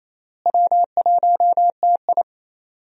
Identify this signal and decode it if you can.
Morse code, heard as a keyed tone, about 28 wpm, 700 Hz. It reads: W1TI
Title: USA Call Signs - 4 Characters - Repeated 2x - 28wpm